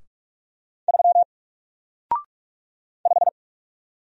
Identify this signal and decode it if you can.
Morse code, heard as a keyed tone, about 44 wpm, 700 Hz. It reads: 3 5
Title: Single Character (Letter:Number:Punctuation:Prosign) Alt 45wpm